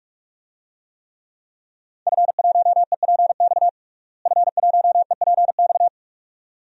Morse code, heard as a keyed tone, about 41 wpm, 700 Hz. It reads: F1EPX F1EPX
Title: DXCC-100 Call Signs - 5 Characters - Repeated 2x - 45wpm